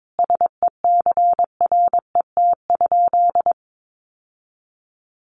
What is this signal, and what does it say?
Morse code, heard as a keyed tone, about 22 wpm, 700 Hz. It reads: SECRET?
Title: Sentences from Top 1200 Words 22wpm